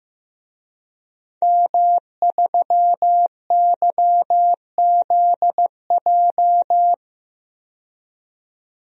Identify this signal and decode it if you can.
Morse code, heard as a keyed tone, about 15 wpm, 700 Hz. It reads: M3YZJ
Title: DXCC-100 Call Signs - 5 Characters - 15wpm